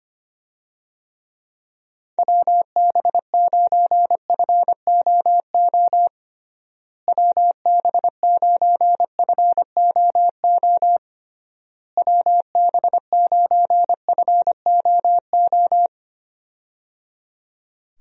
Morse code, heard as a keyed tone, about 25 wpm, 700 Hz. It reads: WB9FOO WB9FOO WB9FOO
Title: USA Call Signs - 6 Characters - Repeated 3x - 25wpm